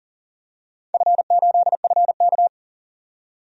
Morse code, heard as a keyed tone, about 40 wpm, 690 Hz. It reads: F8FK
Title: DXCC-100 Call Signs - 4 Characters - 40wpm